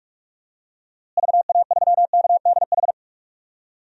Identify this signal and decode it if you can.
Morse code, heard as a keyed tone, about 43 wpm, 700 Hz. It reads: VA3KDH
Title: DXCC-100 Call Signs - 6 Characters - Repeated 2x - 45wpm